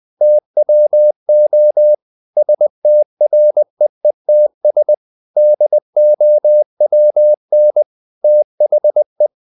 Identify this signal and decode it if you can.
Morse code, heard as a keyed tone, about 20 wpm, 600 Hz. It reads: TWO STREETS DOWN THE
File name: the old man and the sea / the old man and the sea part 1 20wpm